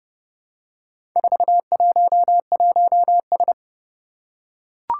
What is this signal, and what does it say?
Morse code, heard as a keyed tone, about 30 wpm, 700 Hz. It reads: 411S
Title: Four Challenging Letter-Number Combinations 30wpm